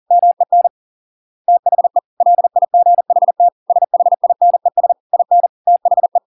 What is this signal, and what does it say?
Morse code, heard as a keyed tone, about 40 wpm, 700 Hz. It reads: MEN THE LIGHT SHINES IN THE